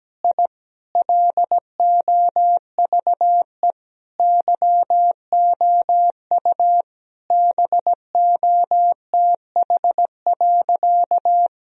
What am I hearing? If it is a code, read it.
Morse code, heard as a keyed tone, about 17 wpm, 700 Hz. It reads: I LOVE YOU BOTH.
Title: Sentences from Top 1000 Words - Review Part 1 - 17wpm